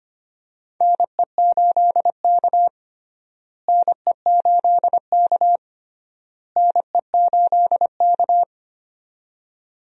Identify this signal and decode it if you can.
Morse code, heard as a keyed tone, about 25 wpm, 700 Hz. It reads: NE8K NE8K NE8K
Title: USA Call Signs - 4 Characters - Repeated 3x - 25wpm